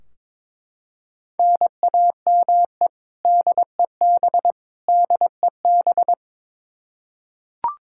Morse code, heard as a keyed tone, about 23 wpm, 700 Hz. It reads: NAME DEB DEB
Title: Practice QSO - Exchange 1 - Beginning - 22wpm